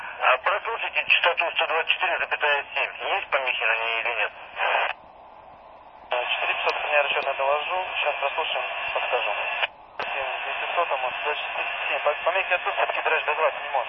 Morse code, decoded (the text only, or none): none